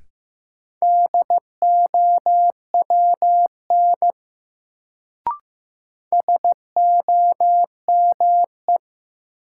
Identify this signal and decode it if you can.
Morse code, heard as a keyed tone, about 15 wpm, 700 Hz. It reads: DOWN SOME
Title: Sets of 2 Words - New Word Drill - 15wpm